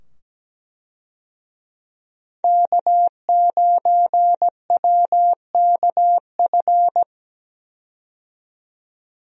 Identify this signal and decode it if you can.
Morse code, heard as a keyed tone, about 17 wpm, 700 Hz. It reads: K9WKF
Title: USA Call Signs - 5 Characters - 17wpm